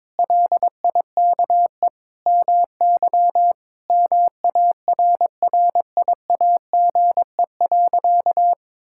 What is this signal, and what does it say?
Morse code, heard as a keyed tone, about 22 wpm, 700 Hz. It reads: LIKE MY MARRIAGE.
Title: Sentences from Top 1000 Words - Review Part 5 - 22wpm